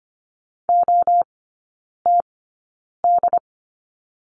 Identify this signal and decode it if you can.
Morse code, heard as a keyed tone, about 24 wpm, spaced as for 9 wpm, 690 Hz. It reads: OTD
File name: Lesson 15e - Number 5 - Three Characters 25wpm